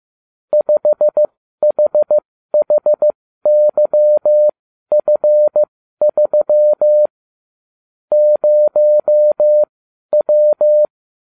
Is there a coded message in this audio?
Yes: Morse code from 0.5 to 10.9 s, heard as a keyed tone, about 15 wpm, spaced as for 12 wpm, 600 Hz. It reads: 5HHYF3 0W